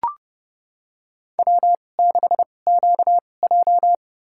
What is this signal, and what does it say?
Morse code, heard as a keyed tone, about 29 wpm, spaced as for 21 wpm, 700 Hz. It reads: W6QJ
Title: USA Call Signs - 4 Characters - 2x Character Spacing - 30wpm